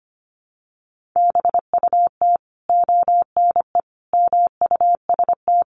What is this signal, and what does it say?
Morse code, heard as a keyed tone, about 25 wpm, 700 Hz. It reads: BUT ONE MUST
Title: Sentences from Top 1100 Words 25wpm